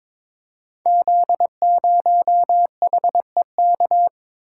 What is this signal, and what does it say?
Morse code, heard as a keyed tone, about 22 wpm, 690 Hz. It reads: Z0HEK
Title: Code Groups - 5 Letter:Numbers - 22wpm